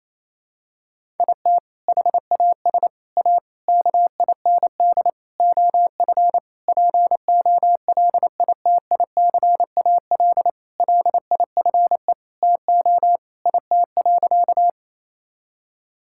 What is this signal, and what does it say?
Morse code, heard as a keyed tone, about 27 wpm, 700 Hz. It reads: IT HAS A KIND OF POLITICAL LIFE TO IT.
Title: Sentences from Top 1000 Words - Review Part 5 - 28wpm